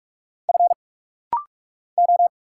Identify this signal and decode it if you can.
Morse code, heard as a keyed tone, about 46 wpm, 700 Hz. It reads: F O
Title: Single Letters - Rapid-Fire 45wpm